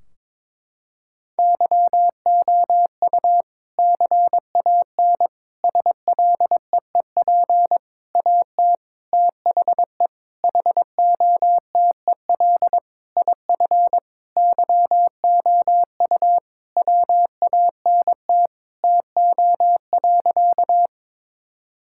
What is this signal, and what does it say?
Morse code, heard as a keyed tone, about 22 wpm, 700 Hz. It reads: YOU CAN SLEEP AT THE HOTEL IF YOU WANT TO.
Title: Sentences from Top 1600 Words 22wpm